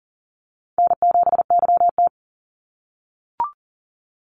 Morse code, heard as a keyed tone, about 40 wpm, 700 Hz. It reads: N7YT
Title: Easy Call Signs - 40wpm